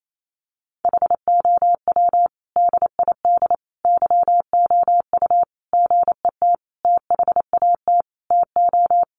Morse code, heard as a keyed tone, about 28 wpm, 700 Hz. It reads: HOW DID YOU GET THAT TO
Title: Sentences from Top 900 Words 28wpm